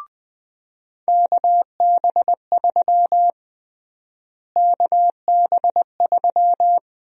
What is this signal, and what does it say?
Morse code, heard as a keyed tone, about 21 wpm, 700 Hz. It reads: KB3 KB3
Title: USA Call Sign Prefixes - Repeated 2x - 20wpm